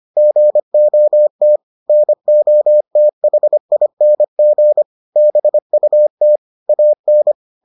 Morse code, heard as a keyed tone, about 25 wpm, 600 Hz. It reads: GOT NOTHING BUT AN